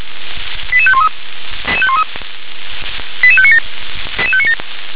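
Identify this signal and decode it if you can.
Five-tone selective call (ZVEI), heard as quick runs of five tones, tones of about 70 ms. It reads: 88513 88597 88597
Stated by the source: ZVEI five-tone selective call received on an SDR.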